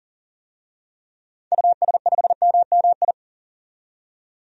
Morse code, heard as a keyed tone, about 40 wpm, 700 Hz. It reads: US5MMI
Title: DXCC-100 Call Signs - 6 Characters - 40wpm